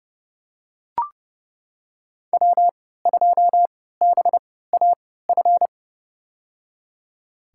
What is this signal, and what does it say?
Morse code, heard as a keyed tone, about 30 wpm, spaced as for 17 wpm, 700 Hz. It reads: W2BAF
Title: USA Call Signs - 5 Characters - 3x Character Spacing - 30wpm